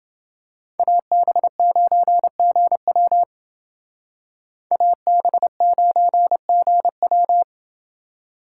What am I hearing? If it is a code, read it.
Morse code, heard as a keyed tone, about 29 wpm, 700 Hz. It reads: AB9GW AB9GW